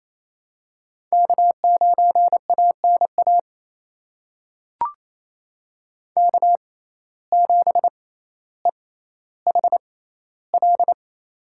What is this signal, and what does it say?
Morse code, heard as a keyed tone, about 28 wpm, spaced as for 14 wpm, 700 Hz. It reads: K9ANA K7EHL
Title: USA Call Signs - 5 Characters - 6x Character Spacing - 28wpm